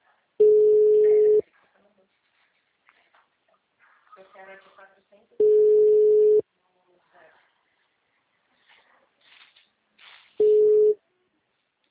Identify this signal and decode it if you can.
Ringback tone (Europe, 425 Hz), with speech.
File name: out- -2004- - - .22